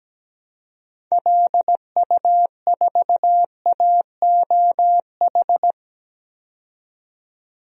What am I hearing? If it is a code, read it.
Morse code, heard as a keyed tone, about 17 wpm, 700 Hz. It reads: LU4AOH
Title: DXCC-100 Call Signs - 6 Characters - 17wpm